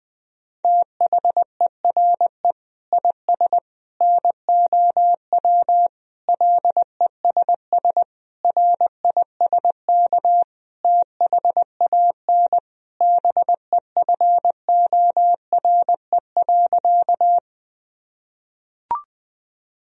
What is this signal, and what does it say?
Morse code, heard as a keyed tone, about 20 wpm, 700 Hz. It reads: THERE IS NOW LESS RISK THAN BEFORE.
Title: Sentences from Top 1000 Words - Review Part 4 - 20wpm